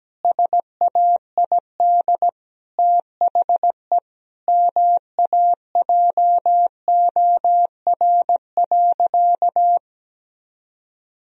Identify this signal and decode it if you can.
Morse code, heard as a keyed tone, about 17 wpm, 700 Hz. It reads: SAID THE MAJOR.
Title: Sentences from Top 1000 Words - Review Part 4 - 17wpm